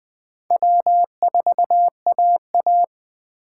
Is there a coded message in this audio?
Yes: Morse code from 0.5 to 2.8 s, heard as a keyed tone, about 19 wpm, 700 Hz. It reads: W4AA